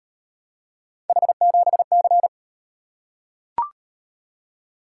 Morse code, heard as a keyed tone, about 38 wpm, 700 Hz. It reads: H7C